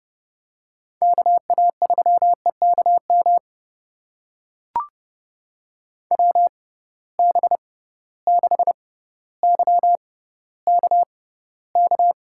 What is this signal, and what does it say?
Morse code, heard as a keyed tone, about 29 wpm, spaced as for 15 wpm, 700 Hz. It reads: KA3EKM WB6YKK
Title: USA Call Signs - 6 Characters - 6x Character Spacing - 30wpm